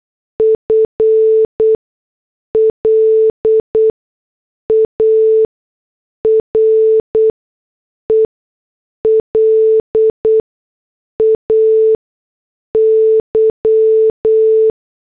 Morse code, heard as a keyed tone, about 8 wpm, spaced as for 6 wpm, 440 Hz. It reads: FLARE LAY